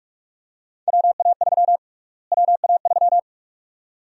Morse code, heard as a keyed tone, about 46 wpm, 700 Hz. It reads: WA3 WA3
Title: USA Call Sign Prefixes - Repeated 2x - 45wpm